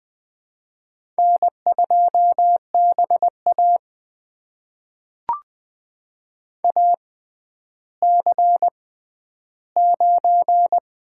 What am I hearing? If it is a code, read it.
Morse code, heard as a keyed tone, about 20 wpm, spaced as for 11 wpm, 700 Hz. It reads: N2BA AC9
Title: USA Call Signs - 4 Characters - 6x Character Spacing - 20wpm